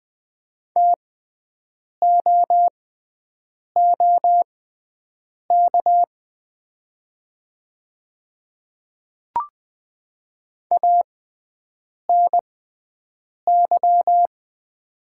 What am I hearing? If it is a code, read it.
Morse code, heard as a keyed tone, about 20 wpm, spaced as for 7 wpm, 700 Hz. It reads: TOOK ANY
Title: 6x Character Spacing - Top 300 Words - 20wpm